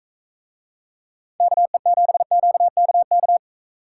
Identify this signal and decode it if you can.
Morse code, heard as a keyed tone, about 42 wpm, 700 Hz. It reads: KE7QKK